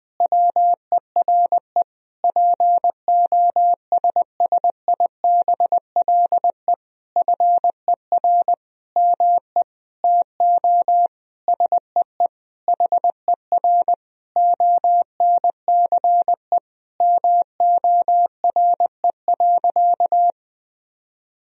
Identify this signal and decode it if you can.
Morse code, heard as a keyed tone, about 20 wpm, 700 Hz. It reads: WERE POSSIBLE FER ME TO SEE HER ONCE MORE.
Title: Sentences from Top 500 Words and Abbreviations 20wpm